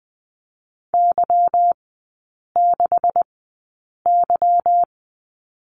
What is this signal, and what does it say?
Morse code, heard as a keyed tone, about 20 wpm, spaced as for 8 wpm, 690 Hz. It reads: Y6Y
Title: 145 Lesson 24e - Letter y - Three Characters 20wpm